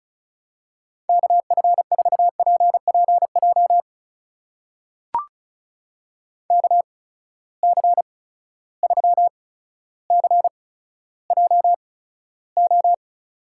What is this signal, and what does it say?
Morse code, heard as a keyed tone, about 36 wpm, spaced as for 15 wpm, 700 Hz. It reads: KF4PPJ KC3CJO
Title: USA Call Signs - 6 Characters - 8x Character Spacing - 35wpm